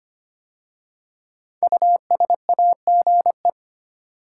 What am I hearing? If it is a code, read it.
Morse code, heard as a keyed tone, about 25 wpm, 700 Hz. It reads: USAGE
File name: 5 Letter Words - First 3 Letters Given - 25wpm